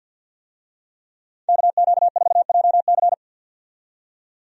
Morse code, heard as a keyed tone, about 48 wpm, 700 Hz. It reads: KQ4JC
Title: USA Call Signs - 5 Characters - 50wpm